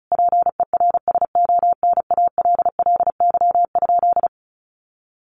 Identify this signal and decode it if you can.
Morse code, heard as a keyed tone, about 34 wpm, 700 Hz. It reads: PERSONALLY?
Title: Sentences from Top 1400 Words 35wpm